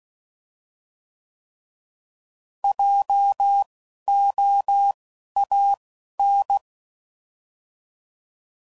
Morse code, heard as a keyed tone, about 16 wpm, spaced as for 11 wpm, 800 Hz. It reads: JOAN